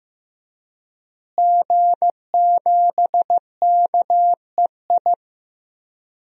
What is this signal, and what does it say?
Morse code, heard as a keyed tone, about 15 wpm, 700 Hz. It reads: G7KEI
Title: DXCC-100 Call Signs - 5 Characters - Repeated 3x - 15wpm